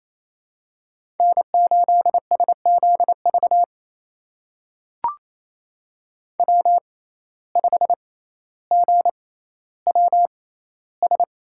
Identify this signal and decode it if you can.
Morse code, heard as a keyed tone, about 28 wpm, spaced as for 14 wpm, 700 Hz. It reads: N8SZV W5GWS
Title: USA Call Signs - 5 Characters - 6x Character Spacing - 28wpm